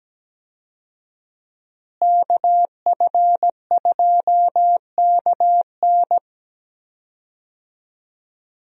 Morse code, heard as a keyed tone, about 17 wpm, 700 Hz. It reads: KF2KN
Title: USA Call Signs - 5 Characters - Repeated 2x - 17wpm